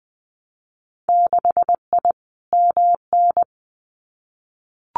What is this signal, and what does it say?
Morse code, heard as a keyed tone, about 20 wpm, 700 Hz. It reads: 6I MN
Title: Winter Field Day Exchanges Encore 20wpm